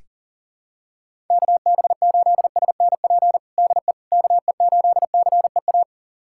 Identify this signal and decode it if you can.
Morse code, heard as a keyed tone, about 40 wpm, 700 Hz. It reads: KB8SNP DE KE8CEA
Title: USA Callsign DE Callsign - 6 Characters - 40wpm